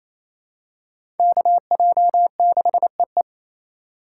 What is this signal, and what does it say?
Morse code, heard as a keyed tone, about 28 wpm, 700 Hz. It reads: KJ6EE
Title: USA Call Signs - 5 Characters - 4x Character Spacing - 28wpm